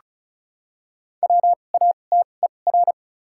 Morse code, heard as a keyed tone, about 36 wpm, spaced as for 25 wpm, 700 Hz. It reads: WATER
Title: 2x Character Spacing - Top 200 Words - 35wpm